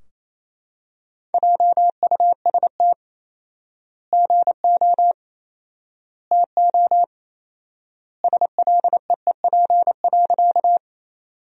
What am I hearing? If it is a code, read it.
Morse code, heard as a keyed tone, about 28 wpm, 700 Hz. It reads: JUST GO TO SLEEP.
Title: 4x Word-Spacing - Sentences from Top 500 Words - 28wpm